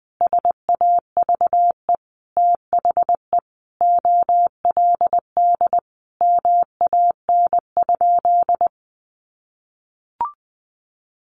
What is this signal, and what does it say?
Morse code, heard as a keyed tone, about 19 wpm, 700 Hz. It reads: SAVE THE OLD MAN?
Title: Sentences from Top 1100 Words 20wpm